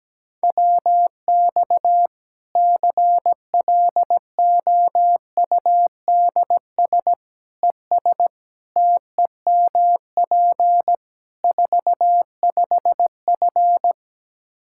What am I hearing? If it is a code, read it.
Morse code, heard as a keyed tone, about 17 wpm, 700 Hz. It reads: WX CLOUDS ES TEMP 45F
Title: Practice QSO - Exchange 2 - Info - 17wpm